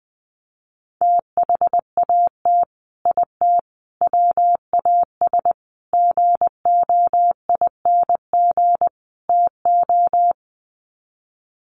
Morse code, heard as a keyed tone, about 20 wpm, 700 Hz. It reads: THAT IT WAS GOING TO
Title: Sets of 5 Words Encore2 20wpm